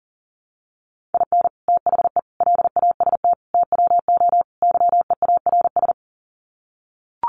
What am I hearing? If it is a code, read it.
Morse code, heard as a keyed tone, about 40 wpm, 700 Hz. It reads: IN THE LAST TWO YEARS
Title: Sets of 5 Words 40wpm